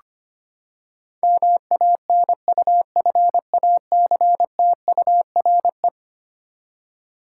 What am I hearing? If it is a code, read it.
Morse code, heard as a keyed tone, about 25 wpm, 700 Hz. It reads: MANUFACTURE
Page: morse-code-ninja-practice-25wpm